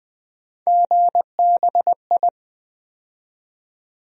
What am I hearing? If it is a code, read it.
Morse code, heard as a keyed tone, about 20 wpm, 700 Hz. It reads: GBI
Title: USA Call Sign Postfixes - 3 Letters 20wpm